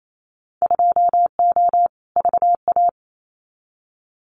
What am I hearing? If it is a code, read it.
Morse code, heard as a keyed tone, about 28 wpm, 700 Hz. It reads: 2O VA
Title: Winter Field Day Exchanges 28wpm